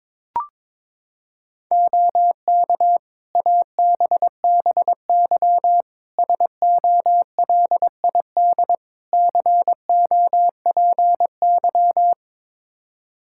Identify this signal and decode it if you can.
Morse code, heard as a keyed tone, about 22 wpm, 700 Hz. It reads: OK ABBY SOLID COPY
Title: Practice QSO - Exchange 3 - More Info - 22wpm